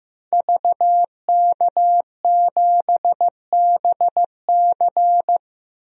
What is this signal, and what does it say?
Morse code, heard as a keyed tone, about 15 wpm, 700 Hz. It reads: VK7BC
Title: DXCC-100 Call Signs - 5 Characters - Repeated 3x - 15wpm